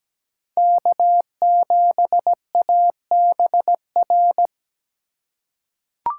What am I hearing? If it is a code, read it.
Morse code, heard as a keyed tone, about 17 wpm, 700 Hz. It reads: K7ABR
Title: USA Call Signs - 5 Characters - 3x Character Spacing - 17wpm